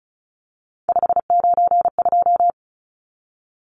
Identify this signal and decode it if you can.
Morse code, heard as a keyed tone, about 34 wpm, 700 Hz. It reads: 592
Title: Numbers - 3-Digit 35wpm